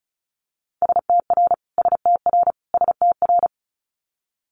Morse code, heard as a keyed tone, about 36 wpm, 700 Hz. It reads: STR STR STR
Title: Multi-letter Phonemes Repeated 3x 35wpm